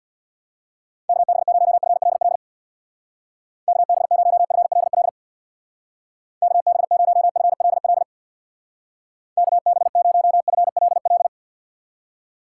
Morse code, heard as a keyed tone, about 66 wpm, 700 Hz. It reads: KB0FLL KB0FLL KB0FLL KB0FLL